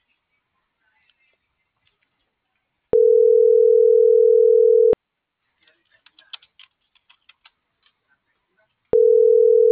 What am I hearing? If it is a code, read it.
Ringback tone (North America), 440 + 480 Hz.